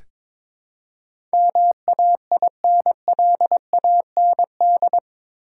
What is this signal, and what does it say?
Morse code, heard as a keyed tone, about 22 wpm, 700 Hz. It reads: MAINLAND
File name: practice / 22wpm / Eight Letter Words - 22wpm